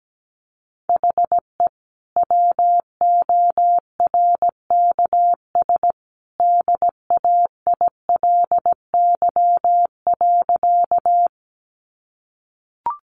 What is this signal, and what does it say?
Morse code, heard as a keyed tone, about 17 wpm, 700 Hz. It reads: HE WORKS DAILY.
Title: Sentences from Top 1100 Words 17wpm